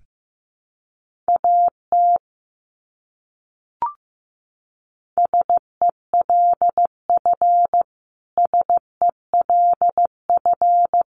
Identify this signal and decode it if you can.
Morse code, heard as a keyed tone, about 15 wpm, 700 Hz. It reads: AT SELF SELF
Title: Top 200 Words Repeated 15wpm